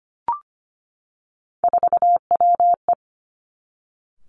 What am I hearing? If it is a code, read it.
Morse code, heard as a keyed tone, about 25 wpm, 700 Hz. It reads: 4WE
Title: Three Challenging Letter-Number Combinations 25wpm